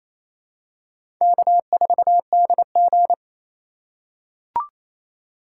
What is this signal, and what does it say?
Morse code, heard as a keyed tone, about 27 wpm, 700 Hz. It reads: K4DG
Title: USA Call Signs - 4x Character Spacing - 28wpm